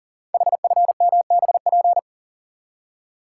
Morse code, heard as a keyed tone, about 38 wpm, 690 Hz. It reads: HFMBP
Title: Code Groups - 5 Letter:Numbers - 40wpm